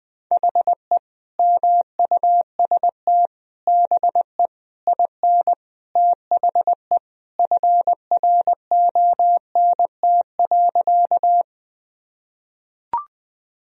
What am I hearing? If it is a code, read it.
Morse code, heard as a keyed tone, about 20 wpm, 700 Hz. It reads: HE MUST BE IN THE FRONT.
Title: Sentences from Top 1000 Words - Review Part 2 - 20wpm